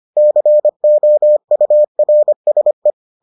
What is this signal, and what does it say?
Morse code, heard as a keyed tone, about 25 wpm, 600 Hz. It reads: COURSE